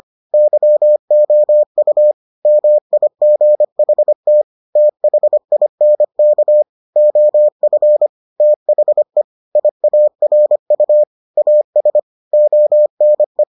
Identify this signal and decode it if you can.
Morse code, heard as a keyed tone, about 25 wpm, 600 Hz. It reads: YOU MIGHT THINK OF THE IARU AS ONE